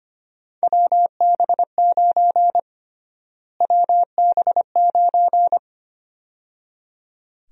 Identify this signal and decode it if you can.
Morse code, heard as a keyed tone, about 25 wpm, 700 Hz. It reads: WB9 WB9
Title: USA Call Sign Prefixes - Repeated 2x - 25wpm